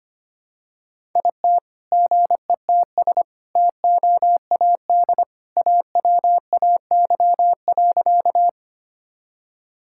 Morse code, heard as a keyed tone, about 25 wpm, 700 Hz. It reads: IT GETS TOAD AWAY.
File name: practice / 25wpm / All-Age Appropriate Jokes - 25wpm